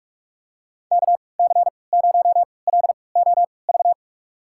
Morse code, heard as a keyed tone, about 43 wpm, spaced as for 25 wpm, 700 Hz. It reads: KC0LOV
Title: USA Call Signs - 6 Characters - 3x Character Spacing - 45wpm